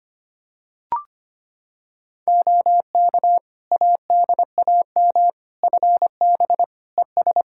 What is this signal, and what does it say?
Morse code, heard as a keyed tone, about 25 wpm, 700 Hz. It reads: OK ADAM FB ES
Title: Practice QSO - Exchange 2 - Info - 25wpm